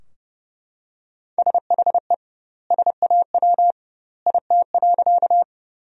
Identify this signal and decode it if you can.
Morse code, heard as a keyed tone, about 29 wpm, 700 Hz. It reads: SHE SAW IT.
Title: 2x Word-Spacing - Sentences from Top 300 Words - 30wpm